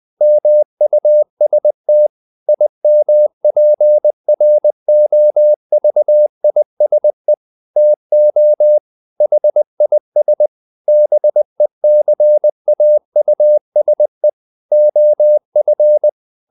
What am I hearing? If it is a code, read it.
Morse code, heard as a keyed tone, about 20 wpm, 600 Hz. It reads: MUST IMPROVISE TO HIS BECAUSE OF
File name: the old man and the sea / the old man and the sea part 1 20wpm